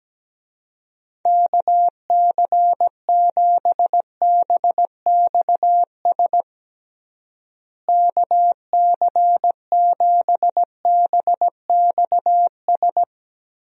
Morse code, heard as a keyed tone, about 17 wpm, 700 Hz. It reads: KC7BXS KC7BXS
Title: USA Call Signs - 6 Characters - Repeated 3x - 17wpm